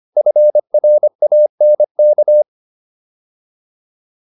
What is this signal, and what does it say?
Morse code, heard as a keyed tone, about 25 wpm, 600 Hz. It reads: FRANK